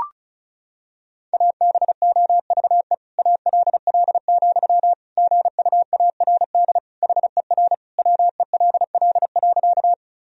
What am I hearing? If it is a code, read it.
Morse code, heard as a keyed tone, about 36 wpm, 700 Hz. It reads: ABOVE ALL, GUARD HER WELL.
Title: Sentences from Top 1000 Words - Review Part 4 - 35wpm